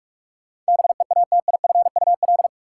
Morse code, heard as a keyed tone, about 46 wpm, 700 Hz. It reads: BEATIFUL
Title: Sentences from Top 1300 Words - New Word Drill Alt - 45wpm